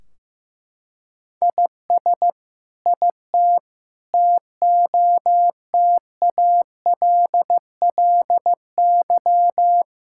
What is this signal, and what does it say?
Morse code, heard as a keyed tone, about 15 wpm, 700 Hz. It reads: IS IT TOTALLY
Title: Sentences from Top 1600 Words 15wpm